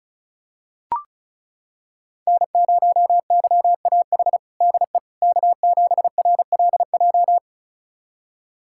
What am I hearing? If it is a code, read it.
Morse code, heard as a keyed tone, about 36 wpm, 700 Hz. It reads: N0YAH DE K7RLJ
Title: USA Callsign DE Callsign - 5 Characters - 35wpm